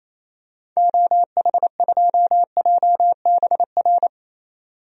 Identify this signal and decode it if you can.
Morse code, heard as a keyed tone, about 27 wpm, 700 Hz. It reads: OH2JBR
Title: DXCC-100 Call Signs - 6 Characters - 28wpm